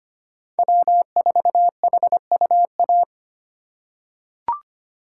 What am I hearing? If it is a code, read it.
Morse code, heard as a keyed tone, about 25 wpm, 700 Hz. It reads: W4HUA
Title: USA Call Signs - 5 Characters - 4x Character Spacing - 25wpm